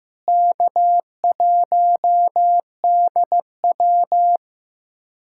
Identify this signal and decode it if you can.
Morse code, heard as a keyed tone, about 15 wpm, 700 Hz. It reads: K1DW
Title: USA Callsign DE Callsign - 4 Characters - 15wpm